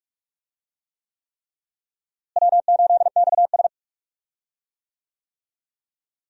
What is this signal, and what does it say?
Morse code, heard as a keyed tone, about 46 wpm, 700 Hz. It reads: W8XS
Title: USA Call Signs - 4 Characters - Repeated 3x - 45wpm